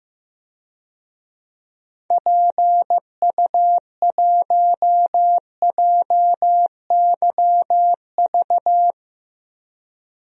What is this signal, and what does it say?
Morse code, heard as a keyed tone, about 15 wpm, 700 Hz. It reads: PU1JYV
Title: DXCC-100 Call Signs - 6 Characters - Repeated 2x - 15wpm